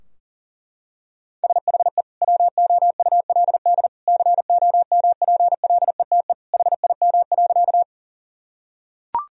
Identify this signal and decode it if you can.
Morse code, heard as a keyed tone, about 40 wpm, 700 Hz. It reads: SHE WOULD COMPLETE HIM.
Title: Sentences from Top 1000 Words - Review Part 2 - 40wpm